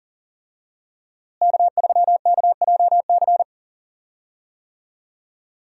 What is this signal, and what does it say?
Morse code, heard as a keyed tone, about 40 wpm, 700 Hz. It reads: K3KJC
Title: USA Call Signs - 5 Characters - Repeated 2x - 40wpm